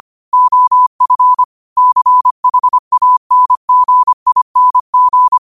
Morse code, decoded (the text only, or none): OF CHANGING